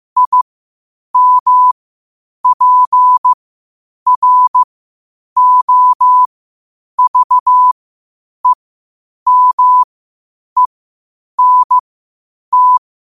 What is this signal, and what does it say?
Morse code, heard as a keyed tone, about 15 wpm, spaced as for 8 wpm, 1000 Hz. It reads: IMPROVEMENT
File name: qotdfiles / 8wpm